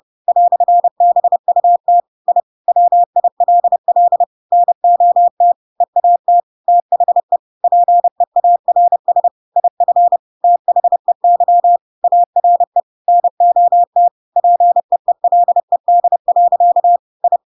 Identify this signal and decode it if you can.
Morse code, heard as a keyed tone, about 30 wpm, 700 Hz. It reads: "BUT I WILL NOT EAT THE PEARS IF THEY ARE NOT PEELED. I
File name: Pinocchio / Pinocchio ch7 30 30 ik1hge